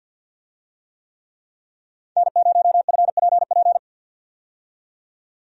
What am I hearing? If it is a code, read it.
Morse code, heard as a keyed tone, about 52 wpm, 700 Hz. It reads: N0FPP
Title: USA Call Signs - 5 Characters - 50wpm